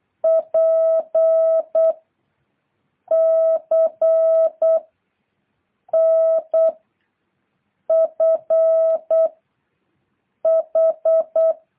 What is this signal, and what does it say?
Morse code, heard as a keyed tone, about 8 wpm, 1300 Hz. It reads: P C N F H